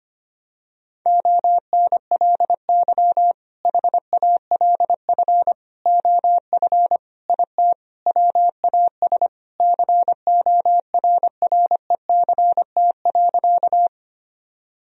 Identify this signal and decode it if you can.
Morse code, heard as a keyed tone, about 25 wpm, 700 Hz. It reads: ONLY HALF OF IT WAS CORRECT.
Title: Sentences from Top 1000 Words - Review Part 2 - 25wpm